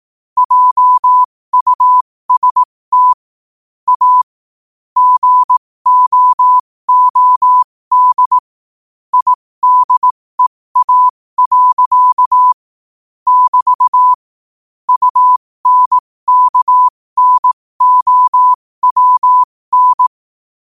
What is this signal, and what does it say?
Morse code, heard as a keyed tone, about 18 wpm, spaced as for 15 wpm, 1000 Hz. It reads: JUST A GOOD IDEA. <BT> UNKNOWN
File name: qotdfiles / 15wpm